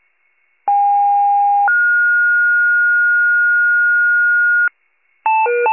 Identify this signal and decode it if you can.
Two-tone page, heard as a one-second tone, then a three-second tone, about 810 then 1430 Hz, with noise. It is heard from 0.7 to 4.7 s.